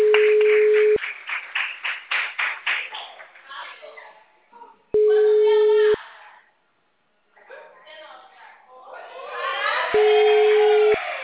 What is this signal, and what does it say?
Ringback tone (Europe, 425 Hz), with speech.